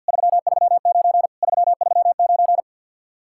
Morse code, heard as a keyed tone, about 62 wpm, 700 Hz. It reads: 339 339
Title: Speed-Racing Common QSO Phrases - 50wpm